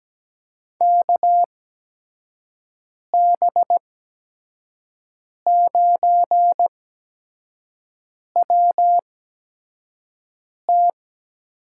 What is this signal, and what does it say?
Morse code, heard as a keyed tone, about 17 wpm, spaced as for 5 wpm, 700 Hz. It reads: KB9WT
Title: USA Call Signs - 6 Characters - 8x Character Spacing - 17wpm